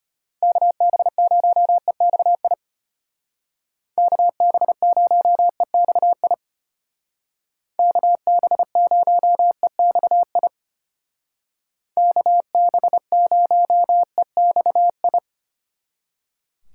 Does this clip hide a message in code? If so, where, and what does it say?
Yes: Morse code from 0.4 to 15.2 s, heard as a keyed tone, about 31 wpm, 700 Hz. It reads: KB0EXI KB0EXI KB0EXI KB0EXI